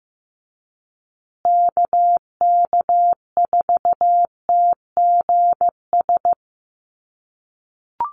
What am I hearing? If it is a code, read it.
Morse code, heard as a keyed tone, about 15 wpm, 700 Hz. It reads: KK4TGS
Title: USA Call Signs Repeated 3x 15wpm